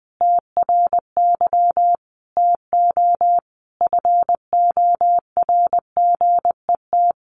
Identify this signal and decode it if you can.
Morse code, heard as a keyed tone, about 20 wpm, 700 Hz. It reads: TRY TO FORGET
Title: Sentences from Top 1200 Words 20wpm